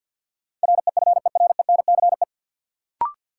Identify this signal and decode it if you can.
Morse code, heard as a keyed tone, about 48 wpm, 700 Hz. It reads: REFERENCE
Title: Sentences from Top 1200 Words - New Word Drill Alt - 50wpm